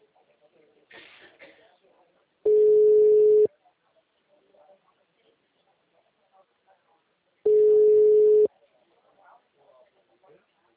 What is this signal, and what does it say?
Ringback tone (Europe, 425 Hz), with speech.